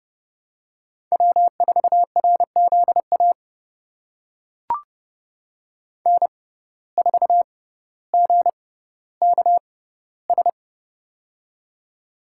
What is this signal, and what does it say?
Morse code, heard as a keyed tone, about 29 wpm, spaced as for 15 wpm, 700 Hz. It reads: W4RZA N4GKS
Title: USA Call Signs - 5 Characters - 6x Character Spacing - 30wpm